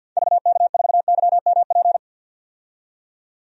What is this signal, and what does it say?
Morse code, heard as a keyed tone, about 53 wpm, 700 Hz. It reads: VK4YMP
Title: DXCC-100 Call Signs - 6 Characters - 50wpm